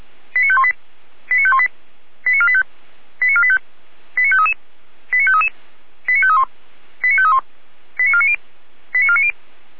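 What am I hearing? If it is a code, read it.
Five-tone selective call (ZVEI), heard as quick runs of five tones, tones of about 70 ms. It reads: 89518 89518 89485 89485 89533 89533 89531 89531 89490 89490
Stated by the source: ZVEI five-tone selective call 89017 received by SDR.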